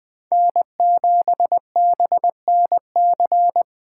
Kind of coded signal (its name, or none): Morse code